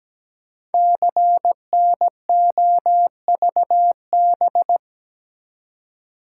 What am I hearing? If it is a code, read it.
Morse code, heard as a keyed tone, about 17 wpm, 700 Hz. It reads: CNOVB
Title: Code Groups - 5 Letters - 17wpm